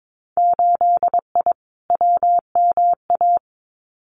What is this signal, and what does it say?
Morse code, heard as a keyed tone, about 22 wpm, 700 Hz. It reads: 8I WMA
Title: Winter Field Day Exchanges Encore 22wpm